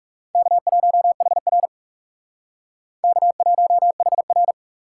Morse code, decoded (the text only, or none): K1HR K1HR